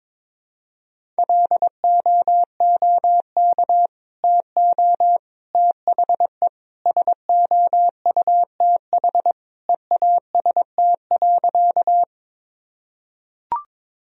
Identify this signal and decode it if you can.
Morse code, heard as a keyed tone, about 22 wpm, 700 Hz. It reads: LOOK TO THE SOUTH EAST.
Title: Sentences from Top 1000 Words - Review Part 3 - 22wpm